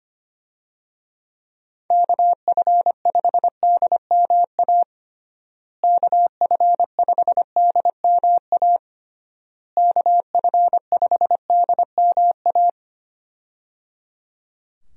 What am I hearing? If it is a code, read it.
Morse code, heard as a keyed tone, about 25 wpm, 700 Hz. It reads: KF5DMA KF5DMA KF5DMA
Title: USA Call Signs - 6 Characters - Repeated 3x - 25wpm